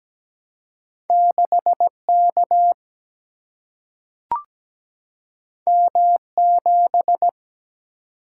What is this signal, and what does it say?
Morse code, heard as a keyed tone, about 17 wpm, 700 Hz. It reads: 6K M7
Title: Single-Letter Single-Number 17wpm